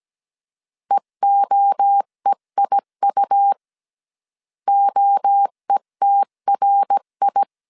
Morse code, heard as a keyed tone, about 17 wpm, 800 Hz. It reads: EOEIU OETRI